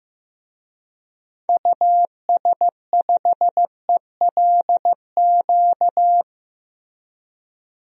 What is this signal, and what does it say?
Morse code, heard as a keyed tone, about 15 wpm, 700 Hz. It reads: US5ELQ
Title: DXCC-100 Call Signs - 6 Characters - Repeated 2x - 15wpm